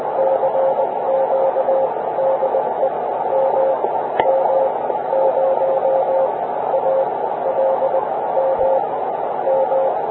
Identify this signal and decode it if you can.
Morse code, heard as a keyed tone, about 20 wpm, spaced as for 14 wpm, 530 Hz, with radio noise. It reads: CQ DE MM0ALM M